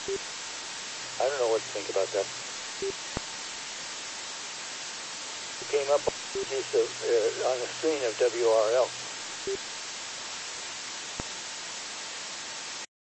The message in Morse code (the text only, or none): none